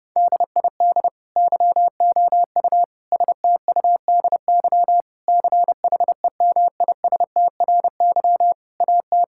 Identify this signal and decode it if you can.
Morse code, heard as a keyed tone, about 28 wpm, 700 Hz. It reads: DID YOU STUDY CHEMISTRY AT
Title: Sentences from Top 1000 Words - Review Part 6 - 30wpm